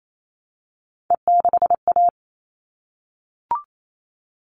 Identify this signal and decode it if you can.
Morse code, heard as a keyed tone, about 28 wpm, 700 Hz. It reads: E6A
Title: All Cut Numbers - 3-Digit 28wpm